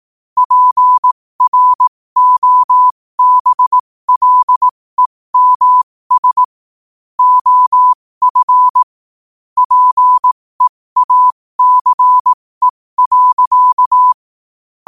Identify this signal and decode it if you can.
Morse code, heard as a keyed tone, about 18 wpm, spaced as for 15 wpm, 1000 Hz. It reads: PROBLEMS OF PEACE.
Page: New quotes every day in morse code at 15 Words per minute.